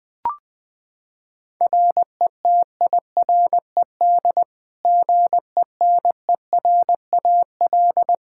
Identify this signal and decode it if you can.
Morse code, heard as a keyed tone, about 20 wpm, 700 Hz. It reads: RETIRED GENERAL
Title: Practice QSO - Exchange 3 - More Info - 20wpm